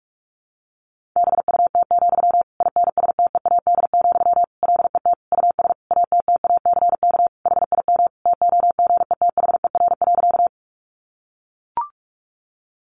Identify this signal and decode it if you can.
Morse code, heard as a keyed tone, about 44 wpm, 700 Hz. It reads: BUT, INSTEAD, LET US ATTACK HIM TOGETHER.
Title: Sentences from Top 1500 Words 45wpm